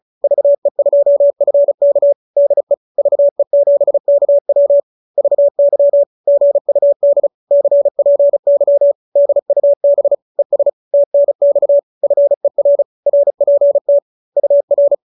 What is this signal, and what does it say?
Morse code, heard as a keyed tone, about 34 wpm, 570 Hz. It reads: VE2FK DE VE7KW VY GUD CPY DUB ES TNX FER RPT UR